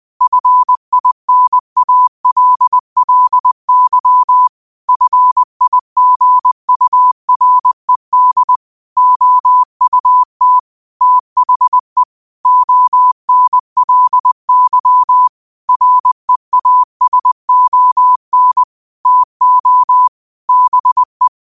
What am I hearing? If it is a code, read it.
Morse code, heard as a keyed tone, about 20 wpm, 1000 Hz. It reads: FINALLY FIGURED OUT THE ONLY REASON TO BE
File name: qotdfiles / 20wpm